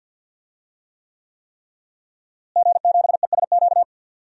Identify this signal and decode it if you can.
Morse code, heard as a keyed tone, about 46 wpm, 700 Hz. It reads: G7ESQ
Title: DXCC-100 Call Signs - 5 Characters - Repeated 3x - 50wpm